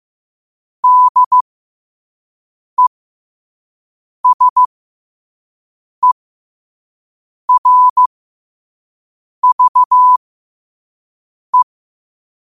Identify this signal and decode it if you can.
Morse code, heard as a keyed tone, about 15 wpm, spaced as for 5 wpm, 1000 Hz. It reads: DESERVE
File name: qotdfiles / 5wpm